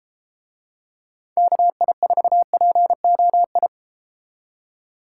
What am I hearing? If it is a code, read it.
Morse code, heard as a keyed tone, about 31 wpm, 700 Hz. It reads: KI4POI